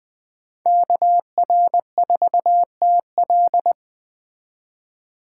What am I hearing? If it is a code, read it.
Morse code, heard as a keyed tone, about 21 wpm, 700 Hz. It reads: KR4TL
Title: Speed Racing - USA Call Signs 17wpm